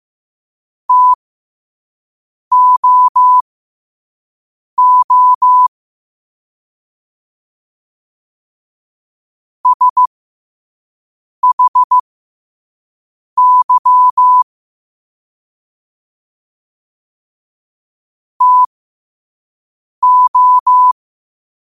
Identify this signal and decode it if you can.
Morse code, heard as a keyed tone, about 15 wpm, spaced as for 5 wpm, 1000 Hz. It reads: TOO SHY TO